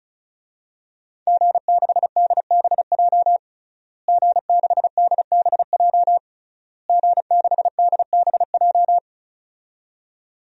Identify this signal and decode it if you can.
Morse code, heard as a keyed tone, about 34 wpm, 700 Hz. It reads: G6DBJ G6DBJ G6DBJ
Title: DXCC-100 Call Signs - 5 Characters - Repeated 3x - 35wpm